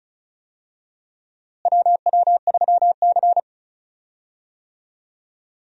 Morse code, heard as a keyed tone, about 36 wpm, 700 Hz. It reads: WW3C
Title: USA Call Signs - 4 Characters - Repeated 3x - 35wpm